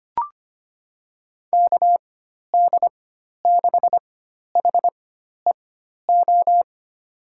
Morse code, heard as a keyed tone, about 25 wpm, spaced as for 12 wpm, 700 Hz. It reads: KD6HEO
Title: USA Call Signs - 6 Characters - 4x Character Spacing - 25wpm